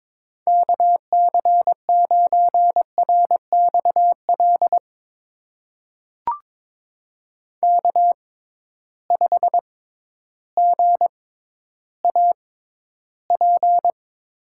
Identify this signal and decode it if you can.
Morse code, heard as a keyed tone, about 22 wpm, spaced as for 12 wpm, 700 Hz. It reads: KC9RXL K5GAP
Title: USA Call Signs - 6x Character Spacing - 22wpm